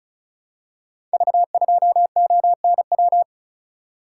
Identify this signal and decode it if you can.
Morse code, heard as a keyed tone, about 35 wpm, 700 Hz. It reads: V2ONW
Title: Code Groups - 5 Letter:Numbers - 35wpm